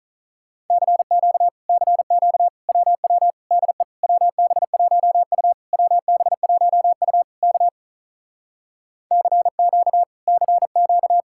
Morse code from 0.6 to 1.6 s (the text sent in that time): CQ